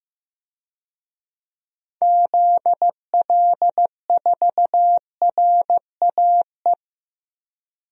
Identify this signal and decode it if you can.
Morse code, heard as a keyed tone, about 16 wpm, 700 Hz. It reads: ZL4RAE
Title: DXCC-100 Call Signs - 6 Characters - Repeated 3x - 15wpm